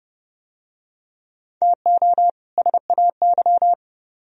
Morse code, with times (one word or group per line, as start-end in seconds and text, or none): 1.6-2.3 TO
2.6-3.7 SAY